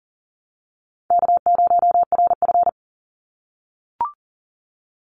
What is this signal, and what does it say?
Morse code, heard as a keyed tone, about 40 wpm, 700 Hz. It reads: K0RF
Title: Call Signs - 40wpm